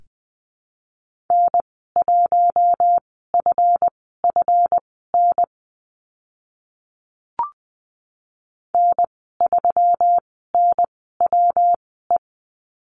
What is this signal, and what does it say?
Morse code, heard as a keyed tone, about 20 wpm, spaced as for 14 wpm, 700 Hz. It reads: N1FFN N3NWE
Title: 140 Lesson 23g - Letter f - Call Signs 20wpm